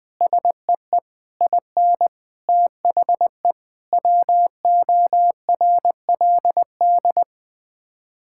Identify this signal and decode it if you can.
Morse code, heard as a keyed tone, about 20 wpm, 700 Hz. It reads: SEE IN THE WORLD
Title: Sets of 9 Words - 20wpm